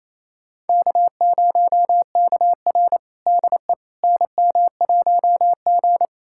Morse code, heard as a keyed tone, about 28 wpm, 700 Hz. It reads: K0KR DE NM1G
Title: USA Callsign DE Callsign - 4 Characters - 28wpm